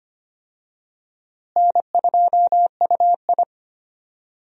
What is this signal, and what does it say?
Morse code, heard as a keyed tone, about 25 wpm, 700 Hz. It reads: N2UI
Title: USA Call Signs - 4 Characters - Repeated 3x - 25wpm